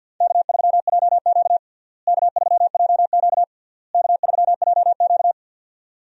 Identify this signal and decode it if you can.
Morse code, heard as a keyed tone, about 52 wpm, 700 Hz. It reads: K3JQ K3JQ K3JQ
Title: USA Call Signs - 4 Characters - Repeated 3x - 50wpm